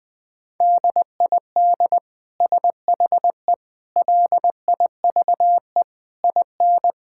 Morse code, heard as a keyed tone, about 21 wpm, 700 Hz. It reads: DID SHE LIVE IN
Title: Sentences from Top 1000 Words - Review Part 5 - 20wpm